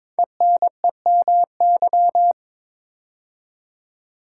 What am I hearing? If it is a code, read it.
Morse code, heard as a keyed tone, about 22 wpm, 700 Hz. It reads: ENEMY
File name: 5 Letter Words - First 3 Letters Given - 22wpm